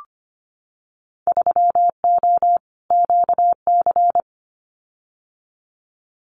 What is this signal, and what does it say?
Morse code, heard as a keyed tone, about 25 wpm, 690 Hz. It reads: 3O QC
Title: Winter Field Day Exchanges Encore 25wpm